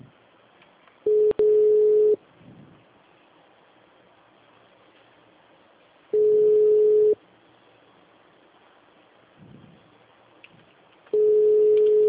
Ringback tone (Europe, 425 Hz).
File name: out- -2013- - - .49